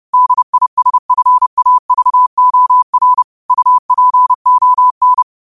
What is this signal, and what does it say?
Morse code, heard as a keyed tone, about 30 wpm, 1000 Hz. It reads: DISFAVOR UPON
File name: qotdfiles / 30wpm